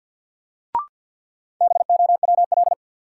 Morse code, heard as a keyed tone, about 51 wpm, 700 Hz. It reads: BOWL
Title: Sentences from Top 1200 Words - New Word Drill Alt - 50wpm